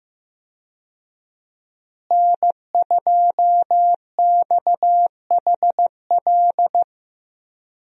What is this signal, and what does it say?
Morse code, heard as a keyed tone, about 15 wpm, 700 Hz. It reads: N2XHL
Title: USA Call Signs - 5 Characters - Repeated 2x - 15wpm